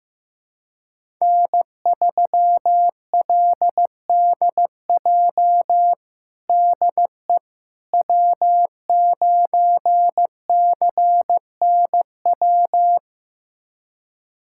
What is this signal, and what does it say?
Morse code, heard as a keyed tone, about 16 wpm, 700 Hz. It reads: N3LDJ DE W9CNW
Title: USA Callsign DE Callsign - 5 Characters - 15wpm